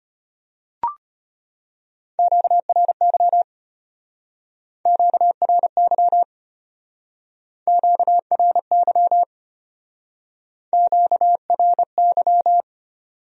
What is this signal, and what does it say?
Morse code, heard as a keyed tone, about 33 wpm, 700 Hz. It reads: QRY QRY QRY QRY